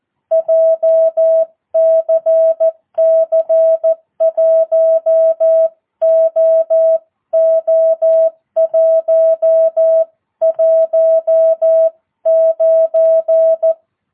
Morse code, heard as a keyed tone, about 14 wpm, 650 Hz. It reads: JCC 1OO119
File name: jh8cbh / 2022cw